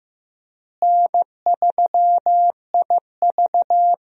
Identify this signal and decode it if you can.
Morse code, heard as a keyed tone, about 15 wpm, 700 Hz. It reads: N3IV
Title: USA Call Signs - 4 Characters - Repeated 2x - 15wpm